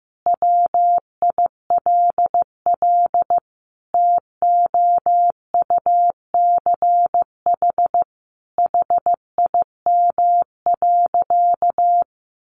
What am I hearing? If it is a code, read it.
Morse code, heard as a keyed tone, about 15 wpm, 700 Hz. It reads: WILL TOUCH HIM.
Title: Sentences from Top 1300 Words 15wpm